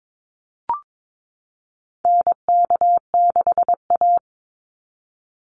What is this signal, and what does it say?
Morse code, heard as a keyed tone, about 22 wpm, 700 Hz. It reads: NK6A
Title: NA SOTA Chaser Callsigns 22wpm